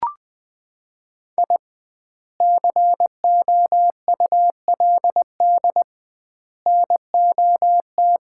Morse code, heard as a keyed tone, about 20 wpm, 700 Hz. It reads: I COULD NOT
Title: 2x Word-Spacing - Sets of 3 Words - 20wpm